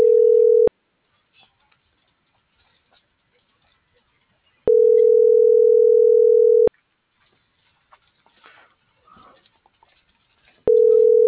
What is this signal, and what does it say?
Ringback tone (North America), 440 + 480 Hz.